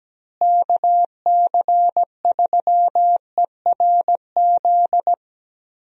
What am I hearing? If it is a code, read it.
Morse code, heard as a keyed tone, about 17 wpm, 700 Hz. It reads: KC3ERZ
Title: USA Call Signs - 6 Characters - Repeated 3x - 17wpm